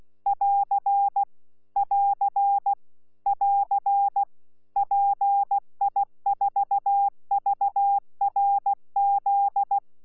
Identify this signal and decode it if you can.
Morse code, heard as a keyed tone, about 16 wpm, 800 Hz. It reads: <AR> <AR> <AR> PI4VRZ
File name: Z PI4VRZ / PI4VRZ 14-jun-2025